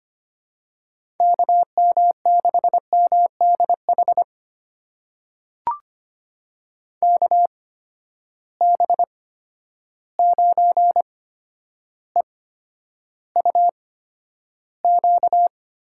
Morse code, heard as a keyed tone, about 25 wpm, spaced as for 11 wpm, 700 Hz. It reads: KM6MDH KB9EUQ
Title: USA Call Signs - 8x Character Spacing - 25wpm